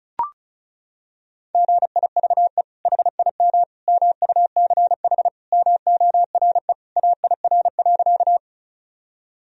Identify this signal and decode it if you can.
Morse code, heard as a keyed tone, about 36 wpm, 700 Hz. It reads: GIVE HIM MUCH MORE AIR.
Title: Sentences from Top 1000 Words - Review Part 1 - 35wpm